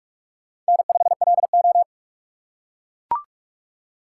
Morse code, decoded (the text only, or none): N5LO